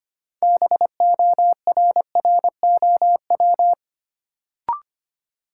Morse code, heard as a keyed tone, about 25 wpm, 700 Hz. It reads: BORROW
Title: Sentences from Top 1500 Words - New Word Drill Alt - 25wpm